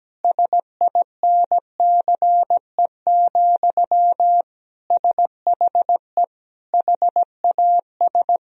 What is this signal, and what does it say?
Morse code, heard as a keyed tone, about 17 wpm, 700 Hz. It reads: SINCE, SHE HAS
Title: Sentences from Top 1000 Words - Review Part 1 - 17wpm